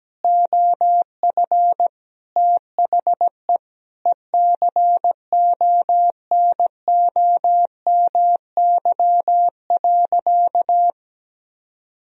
Morse code, heard as a keyed tone, about 17 wpm, 700 Hz. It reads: OF THE ECONOMY.
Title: Sentences from Top 1000 Words - Review Part 3 - 17wpm